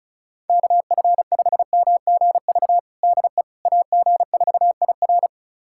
Morse code, heard as a keyed tone, about 34 wpm, 700 Hz. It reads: KF5MGV DE AG4IR
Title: USA Callsign DE Callsign 35wpm